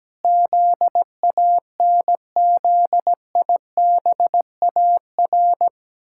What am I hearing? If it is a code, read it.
Morse code, heard as a keyed tone, about 17 wpm, 700 Hz. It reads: ZANZIBAR
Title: Pangram Challenge v2 - 17wpm